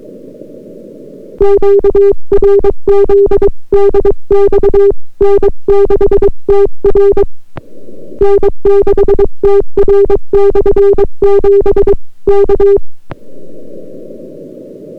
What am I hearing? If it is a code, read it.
Morse code, heard as a keyed tone, about 22 wpm, 400 Hz. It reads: QRZ DX N6TR N6TR/7 K